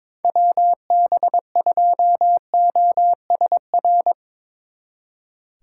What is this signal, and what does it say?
Morse code, heard as a keyed tone, about 23 wpm, 700 Hz. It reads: WB2OSR
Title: Speed Racing - USA 2x3 Call Signs 22wpm